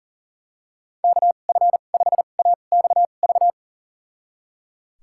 Morse code, heard as a keyed tone, about 37 wpm, spaced as for 28 wpm, 700 Hz. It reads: KF5AXV
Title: USA Call Signs - 2x Character Spacing - 40wpm